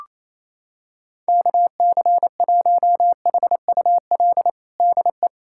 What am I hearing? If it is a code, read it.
Morse code, heard as a keyed tone, about 28 wpm, 700 Hz. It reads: KC1HUL DE
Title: USA Callsign DE Callsign - 6 Characters - 28wpm